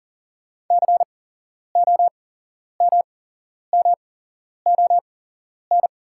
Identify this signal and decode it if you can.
Morse code, heard as a keyed tone, about 40 wpm, spaced as for 11 wpm, 700 Hz. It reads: COMMON
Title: 8x Character Spacing - Top 500 Words - 40wpm